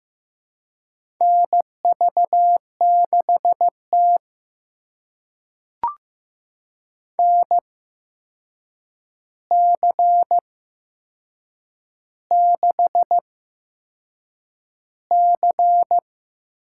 Morse code, heard as a keyed tone, about 15 wpm, spaced as for 6 wpm, 700 Hz. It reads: NV6T NC6C
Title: USA Call Signs - 4 Characters - 8x Character Spacing - 15wpm